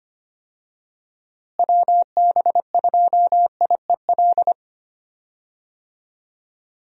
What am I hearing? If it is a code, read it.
Morse code, heard as a keyed tone, about 25 wpm, 700 Hz. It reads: WB2IEL
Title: USA Call Signs - 6 Characters - Repeated 2x - 25wpm